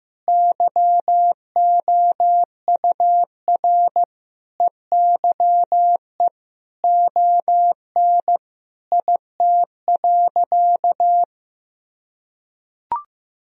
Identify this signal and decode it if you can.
Morse code, heard as a keyed tone, about 15 wpm, 700 Hz. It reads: YOUR EYE ON IT.
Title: Sentences from Top 1000 Words - Review Part 1 - 15wpm